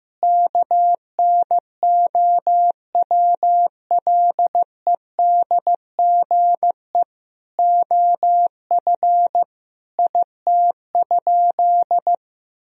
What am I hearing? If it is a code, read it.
Morse code, heard as a keyed tone, about 15 wpm, 700 Hz. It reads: KNOWLEDGE OF IT?
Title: Sentences from Top 1000 Words - Review Part 3 - 15wpm